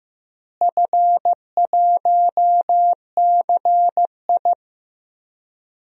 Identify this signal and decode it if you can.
Morse code, heard as a keyed tone, about 15 wpm, 700 Hz. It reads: F1CI
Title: DXCC-100 Call Signs - 4 Characters - Repeated 3x - 15wpm